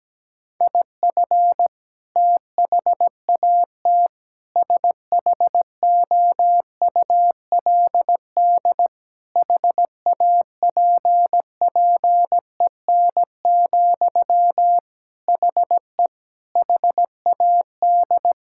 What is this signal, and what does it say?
Morse code, heard as a keyed tone, about 17 wpm, 700 Hz. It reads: IF THAT SHOULD HAPPEN, HE HAD
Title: Sentences from Top 1000 Words - Review Part 6 - 17wpm